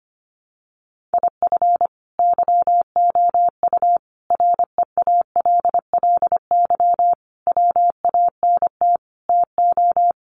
Morse code, heard as a keyed tone, about 25 wpm, 700 Hz. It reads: IF YOU REALLY WANT TO
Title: Sets of 5 Words Encore2 25wpm